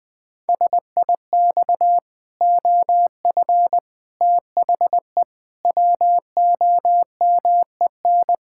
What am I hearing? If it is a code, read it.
Morse code, heard as a keyed tone, about 20 wpm, 700 Hz. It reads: SIX OF THE WOMEN
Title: Pangram Challenge v2 - 20wpm